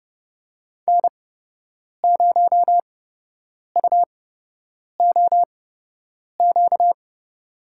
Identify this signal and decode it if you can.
Morse code, heard as a keyed tone, about 30 wpm, spaced as for 8 wpm, 700 Hz. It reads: N0UOQ
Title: USA Call Signs - 8x Character Spacing - 30wpm